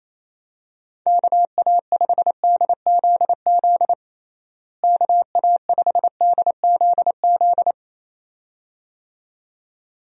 Morse code, heard as a keyed tone, about 28 wpm, 700 Hz. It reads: KA5DZZ KA5DZZ
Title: USA Call Signs - 6 Characters - Repeated 2x - 28wpm